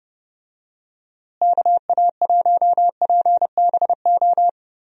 Morse code, heard as a keyed tone, about 30 wpm, 700 Hz. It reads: KA1PBO
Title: USA Call Signs - 6 Characters - Repeated 2x - 30wpm